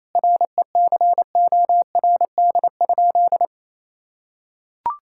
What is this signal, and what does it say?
Morse code, heard as a keyed tone, about 28 wpm, 700 Hz. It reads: RECORD?
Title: Sentences from Top 1000 Words - Review Part 6 - 28wpm